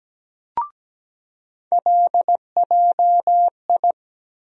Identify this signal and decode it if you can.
Morse code, heard as a keyed tone, about 17 wpm, 700 Hz. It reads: LJI
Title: USA Call Sign Postfixes - 3 Letters 17wpm